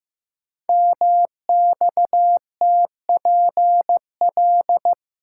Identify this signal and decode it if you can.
Morse code, heard as a keyed tone, about 15 wpm, 700 Hz. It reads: MXTPL
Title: Code Groups - 5 Letters - 15wpm